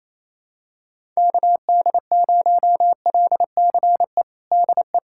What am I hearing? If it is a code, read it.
Morse code, heard as a keyed tone, about 27 wpm, 700 Hz. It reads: KD0LCE DE
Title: USA Callsign DE Callsign - 6 Characters - 28wpm